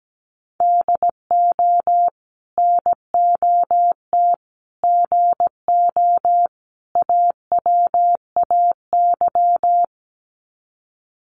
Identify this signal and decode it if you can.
Morse code, heard as a keyed tone, about 17 wpm, 700 Hz. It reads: DO NOT GO AWAY
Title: Sets of 4 Words 17wpm